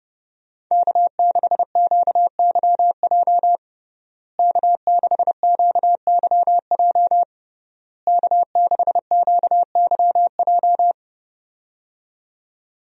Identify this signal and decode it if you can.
Morse code, heard as a keyed tone, about 30 wpm, 700 Hz. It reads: K6QYJ K6QYJ K6QYJ
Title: USA Call Signs - 5 Characters - Repeated 3x - 30wpm